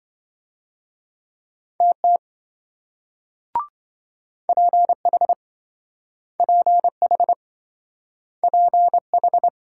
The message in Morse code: TT PH PH PH

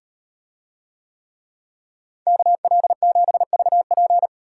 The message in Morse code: KL7VP

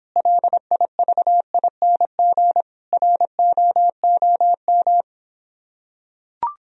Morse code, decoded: LIVING ROOM

